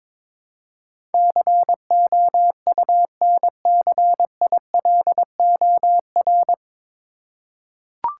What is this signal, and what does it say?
Morse code, heard as a keyed tone, about 22 wpm, 700 Hz. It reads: COUNCILOR
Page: Morse Code Ninja practice 22wpm